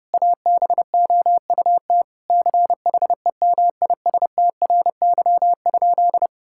Morse code, heard as a keyed tone, about 30 wpm, 700 Hz. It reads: ABOUT CHEMISTRY?
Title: Sentences from Top 1000 Words - Review Part 6 - 30wpm